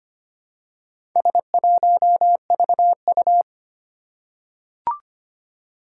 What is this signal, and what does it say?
Morse code, heard as a keyed tone, about 25 wpm, 700 Hz. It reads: S1VU
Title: Four Challenging Letter-Number Combinations 25wpm